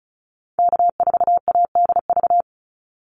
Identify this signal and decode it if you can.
Morse code, heard as a keyed tone, about 34 wpm, 700 Hz. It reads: K4ADV
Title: NA SOTA Activator Callsigns 35wpm